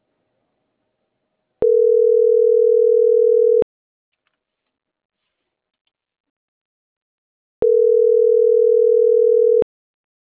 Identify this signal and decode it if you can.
Ringback tone (North America), 440 + 480 Hz.